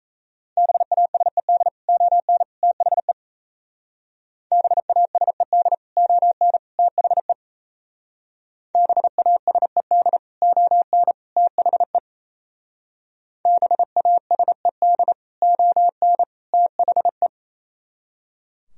Morse code, heard as a keyed tone, about 37 wpm, 700 Hz. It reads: BASED ON THE BASED ON THE BASED ON THE BASED ON THE